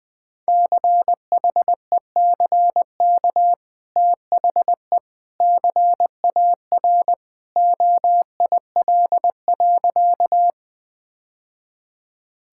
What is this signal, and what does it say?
Morse code, heard as a keyed tone, about 20 wpm, 700 Hz. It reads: CHECK THE CAR OIL.
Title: Easy Sentences with Hints - 20wpm